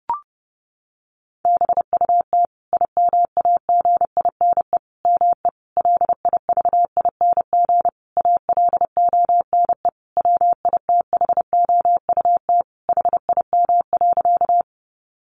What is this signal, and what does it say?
Morse code, heard as a keyed tone, about 30 wpm, 700 Hz. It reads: BUT IMAGINE ME LIVING ALONE WITHOUT HIM.
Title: Sentences from Top 1400 Words 30wpm